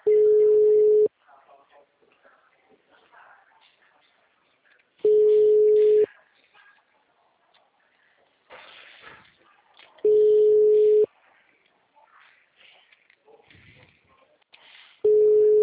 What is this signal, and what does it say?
Ringback tone (Europe, 425 Hz), with speech.